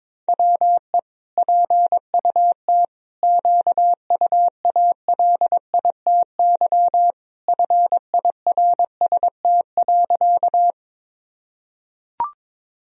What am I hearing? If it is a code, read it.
Morse code, heard as a keyed tone, about 22 wpm, 700 Hz. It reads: WE PUT QUALITY FIRST.
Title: Sentences from Top 1000 Words - Review Part 4 - 22wpm